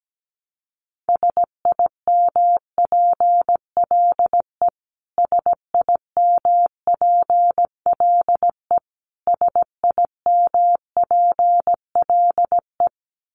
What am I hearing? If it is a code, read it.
Morse code, heard as a keyed tone, about 17 wpm, 700 Hz. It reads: SIMPLE SIMPLE SIMPLE
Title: Top 500 Words Repeated 17wpm